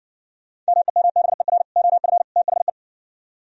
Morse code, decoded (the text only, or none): MEMBER OF THE